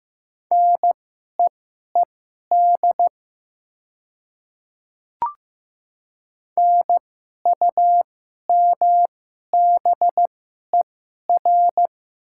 Morse code, heard as a keyed tone, about 15 wpm, spaced as for 11 wpm, 700 Hz. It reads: NEED NUMBER
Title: 2x Character Spacing - Top 200 Words - 15wpm